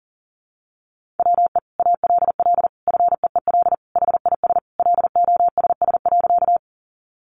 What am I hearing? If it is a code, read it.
Morse code, heard as a keyed tone, about 40 wpm, 700 Hz. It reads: WE ALL FEEL HIS LOSS.
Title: Sentences from Top 1400 Words 40wpm